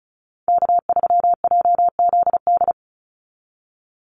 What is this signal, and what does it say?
Morse code, heard as a keyed tone, about 36 wpm, 700 Hz. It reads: K3JZD
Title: NA SOTA Chaser Callsigns 35wpm